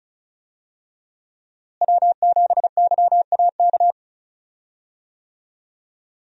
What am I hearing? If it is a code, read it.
Morse code, heard as a keyed tone, about 34 wpm, 700 Hz. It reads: W7YAK
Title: USA Call Signs - 5 Characters - Repeated 3x - 35wpm